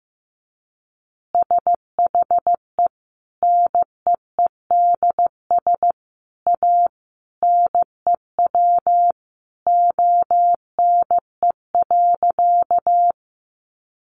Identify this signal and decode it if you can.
Morse code, heard as a keyed tone, about 15 wpm, 700 Hz. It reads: SHE NEEDS A NEW ONE.
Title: Additional Easy Phrases 15wpm